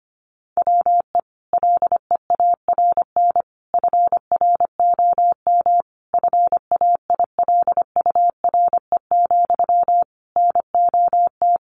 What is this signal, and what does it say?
Morse code, heard as a keyed tone, about 25 wpm, 700 Hz. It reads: WE LEARN FROM FAILURE, NOT
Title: Sentences from Top 900 Words 25wpm